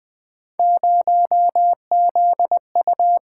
Morse code, heard as a keyed tone, about 20 wpm, 700 Hz. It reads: 0ZU
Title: Three Letter-Number Combinations - 20wpm